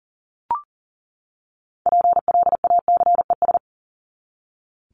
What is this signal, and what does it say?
Morse code, heard as a keyed tone, about 40 wpm, 700 Hz. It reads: PLACES
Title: ES:S Suffix - Top 100 Words with Base Words 40wpm